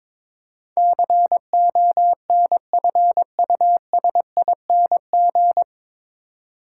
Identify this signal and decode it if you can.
Morse code, heard as a keyed tone, about 22 wpm, 700 Hz. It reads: CONFUSING
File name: 9 Letter Words - First 3 Letters Given - 22wpm